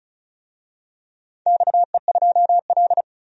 Morse code, heard as a keyed tone, about 34 wpm, 700 Hz. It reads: XE2L
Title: DXCC-100 Call Signs - 4 Characters - 35wpm